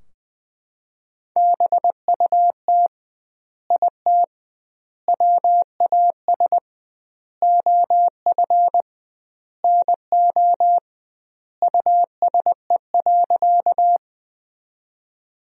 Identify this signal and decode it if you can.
Morse code, heard as a keyed tone, about 21 wpm, 700 Hz. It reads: BUT IT WAS OF NO USE.
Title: 2x Word-Spacing - Sentences from Top 100 Words - 20wpm